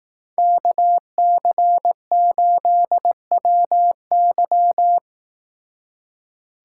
Morse code, heard as a keyed tone, about 18 wpm, 700 Hz. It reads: KC8WY
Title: Speed Racing - USA 2x2 Call Signs 15wpm